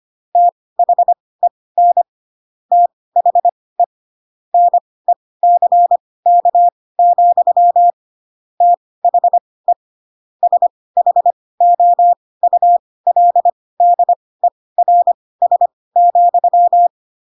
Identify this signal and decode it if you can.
Morse code, heard as a keyed tone, about 25 wpm, spaced as for 18 wpm, 700 Hz. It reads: THEN THE NECK, THE SHOULDERS,
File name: Pinocchio ch3 12 25 ik1hge